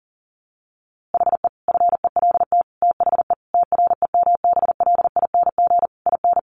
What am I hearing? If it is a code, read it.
Morse code, heard as a keyed tone, about 40 wpm, 700 Hz. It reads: HE FELT THE TREMBLING IN
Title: A Sound Of Thunder 40wpm